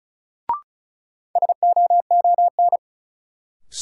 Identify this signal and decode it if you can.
Morse code, heard as a keyed tone, about 36 wpm, 700 Hz. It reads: SOON
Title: Sentences from Top 400 Words - New Word Drill - 35wpm